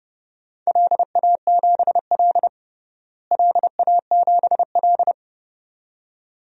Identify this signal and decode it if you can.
Morse code, heard as a keyed tone, about 30 wpm, 700 Hz. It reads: LA7L LA7L
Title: DXCC-100 Call Signs - 4 Characters - Repeated 2x - 30wpm